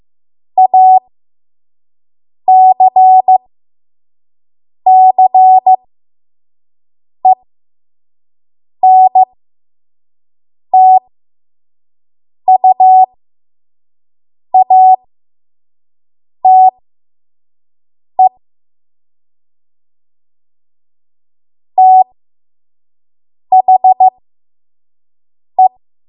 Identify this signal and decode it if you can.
Morse code, heard as a keyed tone, about 15 wpm, spaced as for 5 wpm, 750 Hz. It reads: ACCENTUATE THE
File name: Morse / 05WPM